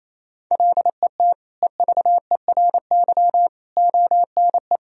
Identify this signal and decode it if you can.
Morse code, heard as a keyed tone, about 28 wpm, 700 Hz. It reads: LET EVERY ONE
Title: Sentences from Top 1000 Words - Review Part 1 - 28wpm